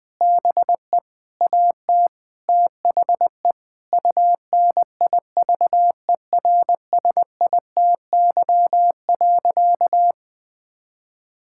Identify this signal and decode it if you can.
Morse code, heard as a keyed tone, about 20 wpm, 700 Hz. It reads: BE AT THE UNIVERSITY.
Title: Sentences from Top 1000 Words - Review Part 5 - 20wpm